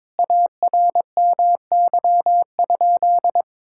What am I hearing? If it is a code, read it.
Morse code, heard as a keyed tone, about 22 wpm, 700 Hz. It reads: ARMY?
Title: Sentences from Top 1000 Words - Review Part 6 - 22wpm